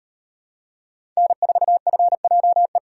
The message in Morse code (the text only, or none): N4FJE